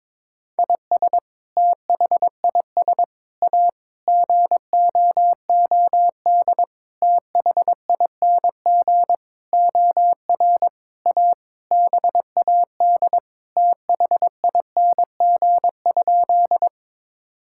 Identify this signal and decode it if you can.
Morse code, heard as a keyed tone, about 22 wpm, 700 Hz. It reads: IS THIS A GOOD THING OR A BAD THING?
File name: Sentences with 7-Word Phrases - 22wpm